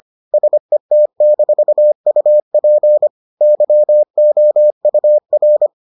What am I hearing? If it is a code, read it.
Morse code, heard as a keyed tone, about 25 wpm, 600 Hz. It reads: SET-UP YOUR